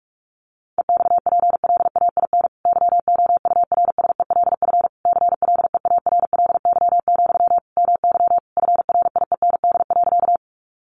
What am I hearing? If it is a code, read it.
Morse code, heard as a keyed tone, about 44 wpm, 700 Hz. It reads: EXPLAIN YOURSELF CLEARLY, MY FRIEND.
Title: Sentences from Top 1300 Words 45wpm